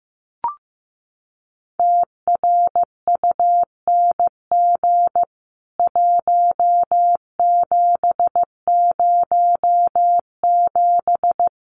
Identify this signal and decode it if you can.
Morse code, heard as a keyed tone, about 15 wpm, 700 Hz. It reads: TRUNG 1707
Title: CWT Exchange 15wpm